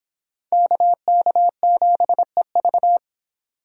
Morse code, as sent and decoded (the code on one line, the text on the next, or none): -.- -.- --... ...-
KK7EV